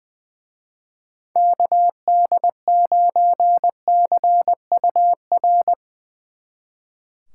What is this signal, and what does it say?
Morse code, heard as a keyed tone, about 21 wpm, 700 Hz. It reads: KD9CUR
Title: Speed Racing - USA 2x3 Call Signs 20wpm